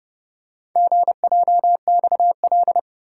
Morse code, heard as a keyed tone, about 28 wpm, 700 Hz. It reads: GJXL